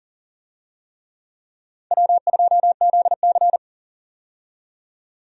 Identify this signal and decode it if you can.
Morse code, heard as a keyed tone, about 40 wpm, 700 Hz. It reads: W2ZC
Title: USA Call Signs - 4 Characters - 40wpm